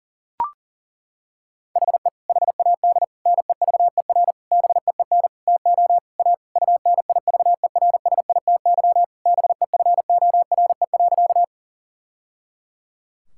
Morse code, heard as a keyed tone, about 37 wpm, 700 Hz. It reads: HE HAD NEVER BEEN TO A UNIVERSITY BEFORE.
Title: Sentences from Top 1000 Words - Review Part 5 - 40wpm